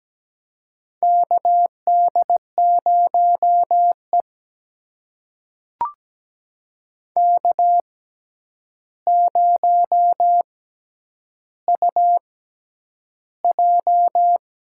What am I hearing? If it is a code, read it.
Morse code, heard as a keyed tone, about 17 wpm, spaced as for 9 wpm, 700 Hz. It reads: KD0E K0UJ
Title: USA Call Signs - 6x Character Spacing - 17wpm